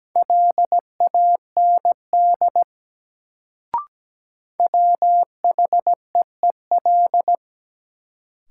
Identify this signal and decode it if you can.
Morse code, heard as a keyed tone, about 17 wpm, 700 Hz. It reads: LAND WHEEL
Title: Sentences from Top 500 Words - New Word Drill - 17wpm